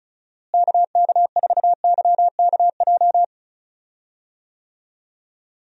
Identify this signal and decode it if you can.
Morse code, heard as a keyed tone, about 36 wpm, 700 Hz. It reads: KK4YKJ
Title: USA Call Signs - 6 Characters - Repeated 2x - 35wpm